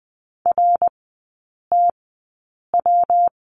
Morse code, heard as a keyed tone, about 20 wpm, spaced as for 8 wpm, 690 Hz. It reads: RTW
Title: 133 Lesson 22e - Question Mark - Three Characters 20wpm